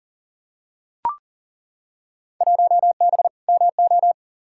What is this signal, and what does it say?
Morse code, heard as a keyed tone, about 39 wpm, 700 Hz. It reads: 1B MO
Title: Field Day Exchanges - 40wpm